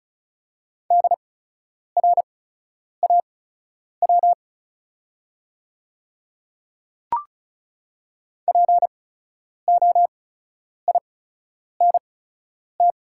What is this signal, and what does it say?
Morse code, heard as a keyed tone, about 36 wpm, spaced as for 10 wpm, 700 Hz. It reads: DRAW POINT
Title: 8x Character Spacing - Top 500 Words - 35wpm